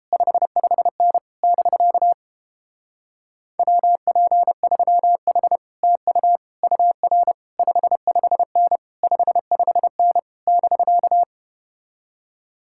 Morse code, heard as a keyed tone, about 31 wpm, 700 Hz. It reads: 55N <BK> WP3H TU UR 55N 55N <BK>